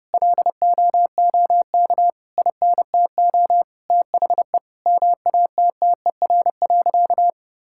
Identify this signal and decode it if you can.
Morse code, heard as a keyed tone, about 30 wpm, 700 Hz. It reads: LOOK INTO THE MATTER.
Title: Sentences from Top 1000 Words - Review Part 6 - 30wpm